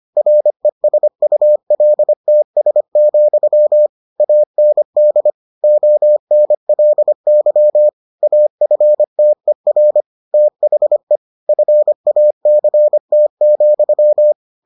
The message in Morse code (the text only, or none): RESULTS, AND ONLY AFTER THE FACT,